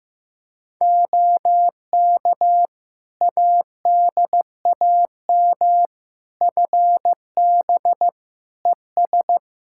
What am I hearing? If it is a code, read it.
Morse code, heard as a keyed tone, about 15 wpm, 700 Hz. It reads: OK ADAM FB ES
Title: Practice QSO - Exchange 2 - Info - 15wpm